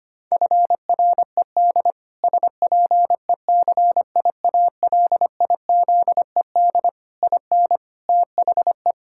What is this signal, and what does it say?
Morse code, heard as a keyed tone, about 25 wpm, 700 Hz. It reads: FRED SPECIALIZED IN THE
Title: Pangram Challenge v2 - 25wpm